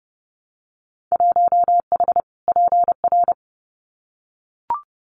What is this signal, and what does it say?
Morse code, heard as a keyed tone, about 30 wpm, 700 Hz. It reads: 1H PR
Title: Winter Field Day Exchanges Encore 30wpm